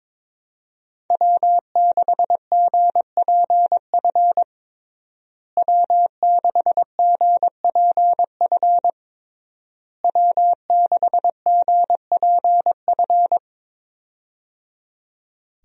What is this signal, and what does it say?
Morse code, heard as a keyed tone, about 21 wpm, 700 Hz. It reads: W6GPF W6GPF W6GPF
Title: USA Call Signs - 5 Characters - Repeated 3x - 22wpm